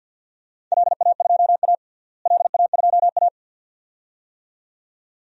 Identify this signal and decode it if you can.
Morse code, heard as a keyed tone, about 49 wpm, 700 Hz. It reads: LA2A LA2A
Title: DXCC-100 Call Signs - 4 Characters - Repeated 2x - 50wpm